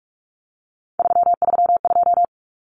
Morse code, heard as a keyed tone, about 44 wpm, 700 Hz. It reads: 332
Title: Numbers - 3-Digit 45wpm